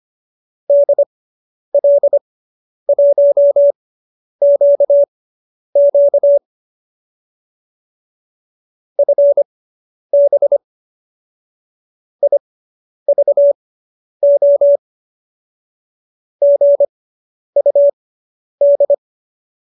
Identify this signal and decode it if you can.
Morse code, heard as a keyed tone, about 25 wpm, spaced as for 10 wpm, 570 Hz. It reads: DL1QQ FB IVO GUD